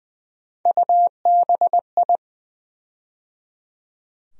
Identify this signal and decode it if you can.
Morse code, heard as a keyed tone, about 21 wpm, 700 Hz. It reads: UBI
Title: USA Call Sign Postfixes - 3 Letters 20wpm